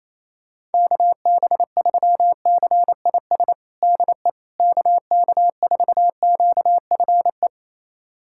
Morse code, heard as a keyed tone, about 28 wpm, 700 Hz. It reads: KB3CIS DE KK4QFE
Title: USA Callsign DE Callsign - 6 Characters - 28wpm